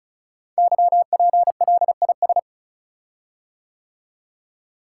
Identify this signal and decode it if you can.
Morse code, heard as a keyed tone, about 36 wpm, 700 Hz. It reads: YPLIS
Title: Code Groups - 5 Letters - 35wpm